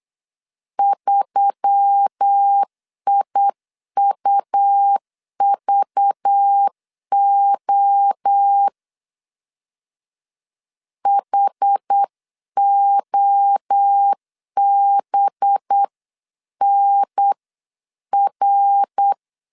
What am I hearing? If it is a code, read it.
Morse code, heard as a keyed tone, about 8 wpm, 800 Hz. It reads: 3IUVO HOBNR
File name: LEZIONI / 6 PKX0 / 6 PKX0 8wpm 3341